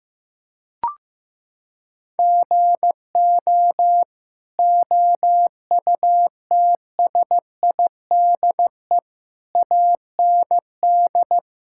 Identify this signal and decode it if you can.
Morse code, heard as a keyed tone, about 16 wpm, 700 Hz. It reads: GO OUTSIDE AND
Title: Sentences from Top 1000 Words - Review Part 6 - 15wpm